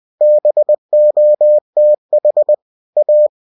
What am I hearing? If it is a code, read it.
Morse code, heard as a keyed tone, about 20 wpm, 600 Hz. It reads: BOTH A
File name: cw-books / winnie-the-pooh / winnie-20wpm-page-0028